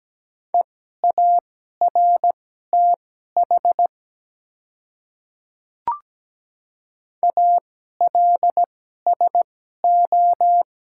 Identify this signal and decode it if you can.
Morse code, heard as a keyed tone, about 17 wpm, spaced as for 12 wpm, 690 Hz. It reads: EARTH ALSO
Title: 2x Character Spacing - Top 300 Words - 17wpm